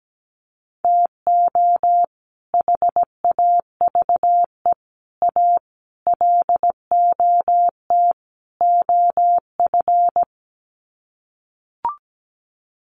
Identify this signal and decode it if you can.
Morse code, heard as a keyed tone, about 17 wpm, 700 Hz. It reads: TO HAVE A LOT OF
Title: Sets of 5 Words Encore 17wpm